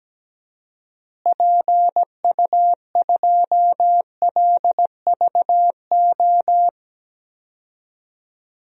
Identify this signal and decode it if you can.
Morse code, heard as a keyed tone, about 17 wpm, 700 Hz. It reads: PU2LVO
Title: DXCC-100 Call Signs - 6 Characters - Repeated 3x - 17wpm